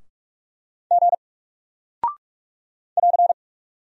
Morse code, heard as a keyed tone, about 44 wpm, 700 Hz. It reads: G <AR>
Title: Single Character (Letter:Number:Punctuation:Prosign) Alt 45wpm